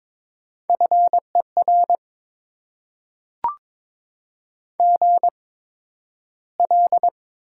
Morse code, heard as a keyed tone, about 22 wpm, spaced as for 12 wpm, 700 Hz. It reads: FER GL
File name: Top 90 QSO Elements - 8x Character Spacing - 22wpm